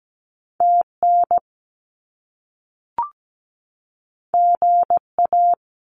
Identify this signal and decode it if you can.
Morse code, heard as a keyed tone, about 17 wpm, 700 Hz. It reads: TN GA
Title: US States 17wpm